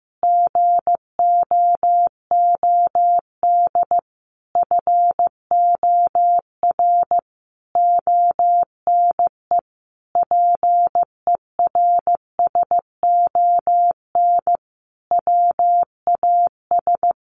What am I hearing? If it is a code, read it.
Morse code, heard as a keyed tone, about 15 wpm, 700 Hz. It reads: GOOD FOR ONE PERSON WAS
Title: Sentences from Top 1400 Words 15wpm